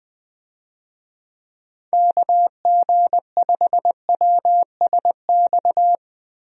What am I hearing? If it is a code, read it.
Morse code, heard as a keyed tone, about 20 wpm, 700 Hz. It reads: KG5WSX
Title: USA Call Signs - 6 Characters - Repeated 3x - 20wpm